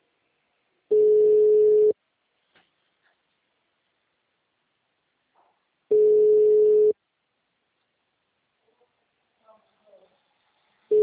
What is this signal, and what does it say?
Ringback tone (Europe, 425 Hz), with speech.
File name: out- -2041- - - .25611